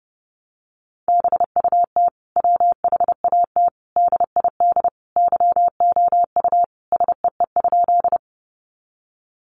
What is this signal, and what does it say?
Morse code, heard as a keyed tone, about 30 wpm, 700 Hz. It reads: BUT WHAT DID YOU SEE?
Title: Sentences from Top 100 Words 30wpm